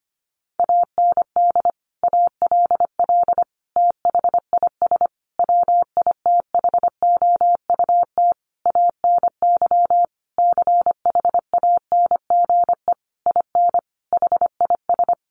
Morse code, heard as a keyed tone, about 25 wpm, 700 Hz. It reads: AND ALL THIS WITHOUT ANY CHANGE IN HIS
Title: Sentences from Top 1200 Words 25wpm